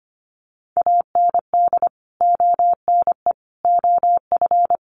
Morse code, heard as a keyed tone, about 25 wpm, 700 Hz. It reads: AND ONE OF
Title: Sets of 5 Words 25wpm